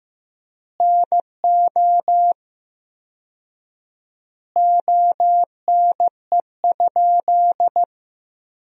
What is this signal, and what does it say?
Morse code, heard as a keyed tone, about 15 wpm, 700 Hz. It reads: NO ONE?
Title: 4x Word-Spacing - Sentences from Top 300 Words - 15wpm